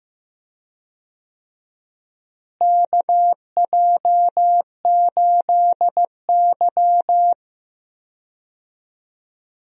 Morse code, heard as a keyed tone, about 15 wpm, 700 Hz. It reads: KJ8Y
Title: USA Call Signs - 4 Characters - Repeated 2x - 15wpm